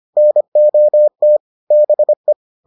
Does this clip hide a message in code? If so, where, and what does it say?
Yes: Morse code from 0.2 to 2.3 s, heard as a keyed tone, about 25 wpm, 600 Hz. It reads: NOT BE